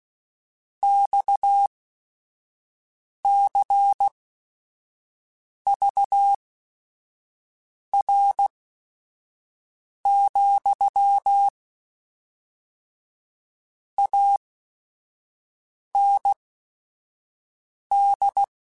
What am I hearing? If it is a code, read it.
Morse code, heard as a keyed tone, about 16 wpm, spaced as for 5 wpm, 800 Hz. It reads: XCVR, AND
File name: QSOs / QSOs-mp3s / WD0FBT 1@5